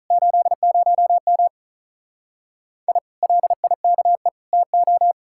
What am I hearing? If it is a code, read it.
Morse code, heard as a keyed tone, about 46 wpm, 700 Hz. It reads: 80M I LIKE TO